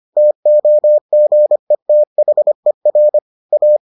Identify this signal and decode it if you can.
Morse code, heard as a keyed tone, about 25 wpm, 600 Hz. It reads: TOGETHER A